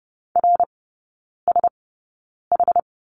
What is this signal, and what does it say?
Morse code, heard as a keyed tone, about 30 wpm, spaced as for 9 wpm, 700 Hz. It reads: RSH
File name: 081 Lesson 14e - Number 2 - Three Characters 30wpm